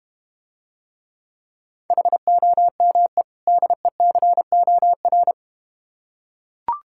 Morse code, heard as a keyed tone, about 31 wpm, 700 Hz. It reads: HOME DECOR
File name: practice / 25wpm / Difficult Words - Speed-Racing with Hints - 25wpm